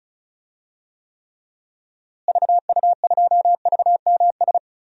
Morse code, heard as a keyed tone, about 36 wpm, 700 Hz. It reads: VU2VMS
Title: DXCC-100 Call Signs - 6 Characters - Repeated 2x - 35wpm